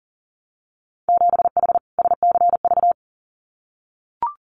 Morse code, heard as a keyed tone, about 41 wpm, 700 Hz. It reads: 7H SCV
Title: Winter Field Day Exchanges Encore 40wpm